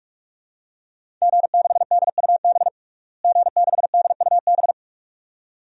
Morse code, heard as a keyed tone, about 46 wpm, 700 Hz. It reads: G6DUB G6DUB
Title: DXCC-100 Call Signs - 5 Characters - Repeated 2x - 45wpm